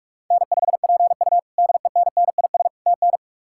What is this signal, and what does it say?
Morse code, heard as a keyed tone, about 47 wpm, 700 Hz. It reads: N5PU DENNIS TN
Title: K1USN SST QSO Practice - 45wpm